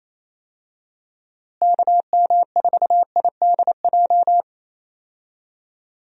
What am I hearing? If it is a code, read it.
Morse code, heard as a keyed tone, about 28 wpm, 700 Hz. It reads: KM4IDJ
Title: USA Call Signs - 6 Characters - Repeated 3x - 28wpm